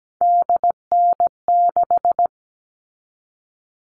Morse code, heard as a keyed tone, about 17 wpm, 700 Hz. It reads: DN6
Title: All Cut Numbers - 3-Digit 17wpm